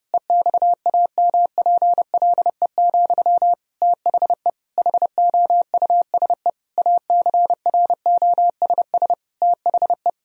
Morse code, heard as a keyed tone, about 30 wpm, 700 Hz. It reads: EXAMPLE, THE HOUSE ACROSS THE
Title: Sentences from Top 1000 Words - Review Part 1 - 30wpm